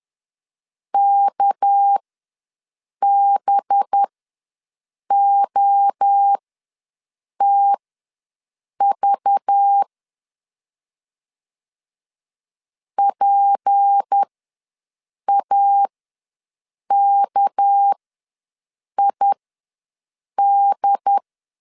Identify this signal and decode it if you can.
Morse code, heard as a keyed tone, about 10 wpm, spaced as for 6 wpm, 800 Hz. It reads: KBOTV PAKID
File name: LEZIONI / 6 PKX0 / 6 PK ws 10wpm 3745